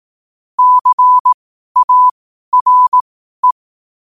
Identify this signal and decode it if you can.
Morse code, heard as a keyed tone, about 18 wpm, spaced as for 12 wpm, 1000 Hz. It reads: CARE
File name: qotdfiles / 12wpm